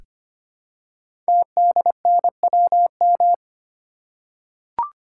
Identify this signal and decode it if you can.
Morse code, heard as a keyed tone, about 25 wpm, 700 Hz. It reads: TDNWM
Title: Code Groups - 5 Letters - 25wpm